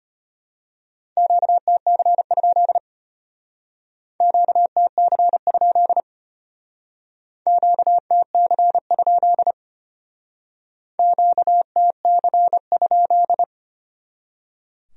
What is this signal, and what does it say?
Morse code, heard as a keyed tone, about 32 wpm, 700 Hz. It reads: QTC? QTC? QTC? QTC?